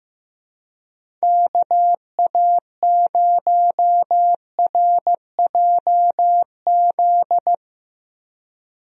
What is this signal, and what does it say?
Morse code, heard as a keyed tone, about 16 wpm, 700 Hz. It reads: KA0RJZ
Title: USA Call Signs - 6 Characters - Repeated 3x - 15wpm